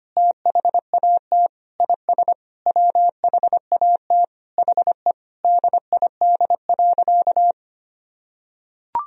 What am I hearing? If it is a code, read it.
Morse code, heard as a keyed tone, about 25 wpm, 700 Hz. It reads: THAT IS WHAT HE DID.
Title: Sentences from Top 1000 Words - Review Part 4 - 25wpm